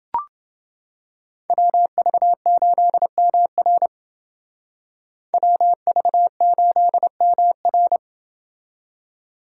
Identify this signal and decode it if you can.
Morse code, heard as a keyed tone, about 28 wpm, 700 Hz. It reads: WV8MR WV8MR